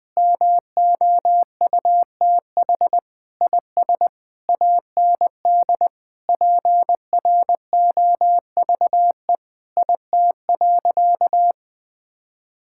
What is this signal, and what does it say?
Morse code, heard as a keyed tone, about 20 wpm, 700 Hz. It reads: MOUTH IS AND PROVE IT.
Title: Sentences with 8-Word Phrases - 20wpm